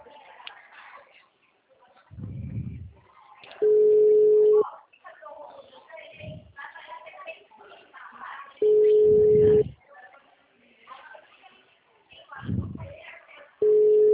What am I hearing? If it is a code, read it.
Ringback tone (Europe, 425 Hz), with speech.